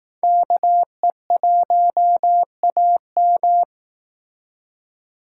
Morse code, heard as a keyed tone, about 18 wpm, 700 Hz. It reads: KE1AM
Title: Speed Racing - USA 2x2 Call Signs 15wpm